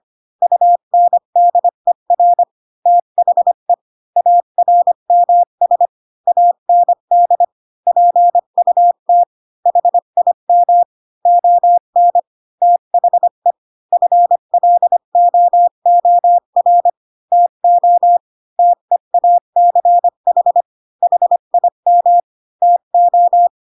Morse code, heard as a keyed tone, about 25 wpm, 700 Hz. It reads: UNDER THE ARMS AND PUT HIM ON THE FLOOR TO TEACH HIM TO
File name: Pinocchio / Pinocchio ch3 20 25 ik1hge